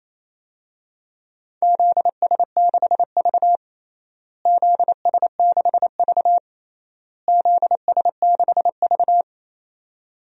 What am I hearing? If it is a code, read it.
Morse code, heard as a keyed tone, about 28 wpm, 700 Hz. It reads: ZS6V ZS6V ZS6V
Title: DXCC-100 Call Signs - 4 Characters - Repeated 3x - 28wpm